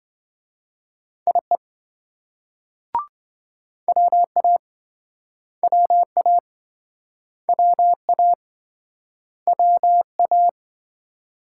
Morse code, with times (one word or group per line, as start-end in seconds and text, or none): 1.3-1.6 IE
3.9-4.6 WA
5.6-6.4 WA
7.5-8.3 WA
9.5-10.5 WA